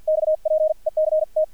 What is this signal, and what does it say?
Morse code, heard as a keyed tone, about 50 wpm, spaced as for 41 wpm, 620 Hz, with radio noise. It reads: XPECT
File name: archived / CRSnet